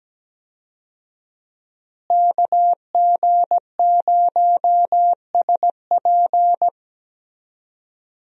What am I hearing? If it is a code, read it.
Morse code, heard as a keyed tone, about 17 wpm, 700 Hz. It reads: KG0SP
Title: USA Call Signs - 5 Characters - Repeated 3x - 17wpm